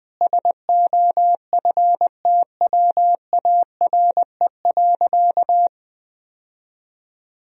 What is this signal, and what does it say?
Morse code, heard as a keyed tone, about 21 wpm, 700 Hz. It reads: SOFTWARE.
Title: Sentences from Top 1000 Words - Review Part 3 - 20wpm